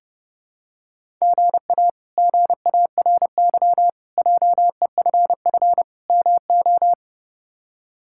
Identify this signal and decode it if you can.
Morse code, heard as a keyed tone, about 32 wpm, 700 Hz. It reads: GA GARY JEFF MO
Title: K1USN SST QSO Practice - 30wpm